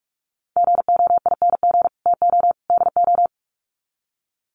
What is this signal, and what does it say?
Morse code, heard as a keyed tone, about 44 wpm, 700 Hz. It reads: GOING TO DO
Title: Sets of 3 Words 45wpm